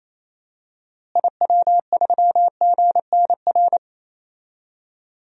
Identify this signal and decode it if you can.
Morse code, heard as a keyed tone, about 28 wpm, 700 Hz. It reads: IW3GNR
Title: DXCC-100 Call Signs - 6 Characters - Repeated 2x - 28wpm